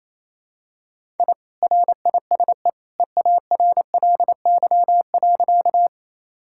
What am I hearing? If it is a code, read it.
Morse code, heard as a keyed tone, about 28 wpm, 700 Hz. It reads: I RISE EARLY.
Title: Sentences from Top 1000 Words - Review Part 6 - 28wpm